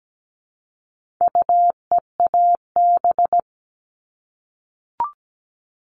Morse code, heard as a keyed tone, about 17 wpm, 700 Hz. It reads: UEAB
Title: All Cut Numbers - 4-Digit 17wpm